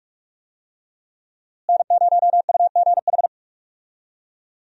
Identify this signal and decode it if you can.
Morse code, heard as a keyed tone, about 46 wpm, 700 Hz. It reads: N0UGH
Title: USA Call Signs - 5 Characters - Repeated 3x - 45wpm